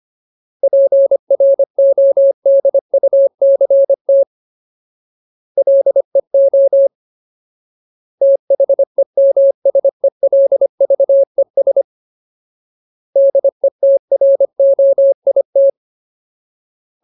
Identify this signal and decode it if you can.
Morse code, heard as a keyed tone, about 25 wpm, 550 Hz. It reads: PRODUCT LEO THEMSELVES DETROIT
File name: Ultimate Word List / 25WPM Ultimate